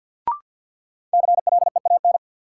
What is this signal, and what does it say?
Morse code, heard as a keyed tone, about 53 wpm, 700 Hz. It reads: CLEAN
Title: Sentences from Top 1100 Words - New Word Drill - 50wpm